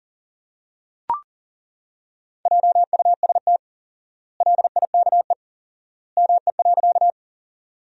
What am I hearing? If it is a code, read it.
Morse code, heard as a keyed tone, about 40 wpm, 700 Hz. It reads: JUST LIKE ME.
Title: 4x Word-Spacing - Sentences from Top 200 Words - 40wpm